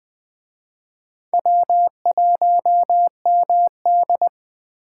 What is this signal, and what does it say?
Morse code, heard as a keyed tone, about 24 wpm, 700 Hz. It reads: W1MD
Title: Speed Racing - USA 1x2 Call Signs 20wpm